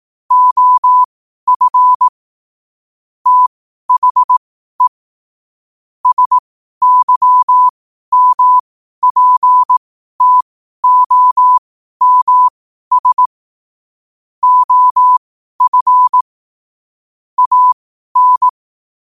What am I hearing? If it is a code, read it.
Morse code, heard as a keyed tone, about 18 wpm, spaced as for 12 wpm, 1000 Hz. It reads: OF THE SYMPTOMS OF AN
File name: qotdfiles / 12wpm